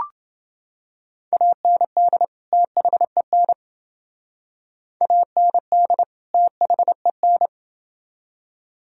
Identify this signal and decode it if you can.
Morse code, heard as a keyed tone, about 29 wpm, 700 Hz. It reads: AND THEN AND THEN